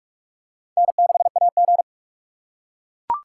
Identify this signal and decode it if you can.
Morse code, heard as a keyed tone, about 41 wpm, 700 Hz. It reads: N6AG
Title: USA Call Signs - 4 Characters - 4x Character Spacing - 45wpm